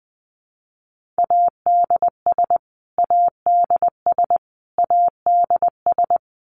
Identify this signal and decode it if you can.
Morse code, heard as a keyed tone, about 20 wpm, 700 Hz. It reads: ADS ADS ADS
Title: Abbreviations Repeated 3x - 20wpm